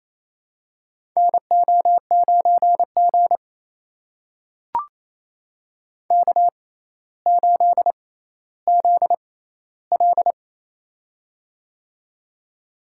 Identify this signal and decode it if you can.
Morse code, heard as a keyed tone, about 28 wpm, spaced as for 14 wpm, 700 Hz. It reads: NO9G K8ZL
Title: USA Call Signs - 4 Characters - 6x Character Spacing - 28wpm